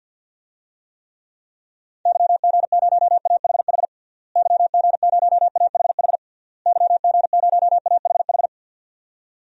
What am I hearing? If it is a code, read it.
Morse code, heard as a keyed tone, about 51 wpm, 700 Hz. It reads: YG0AHH YG0AHH YG0AHH
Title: DXCC-100 Call Signs - 6 Characters - Repeated 3x - 50wpm